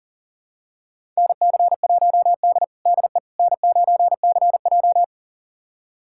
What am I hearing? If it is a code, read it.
Morse code, heard as a keyed tone, about 40 wpm, 700 Hz. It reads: NC1D DE N9CJ
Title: USA Callsign DE Callsign - 4 Characters - 40wpm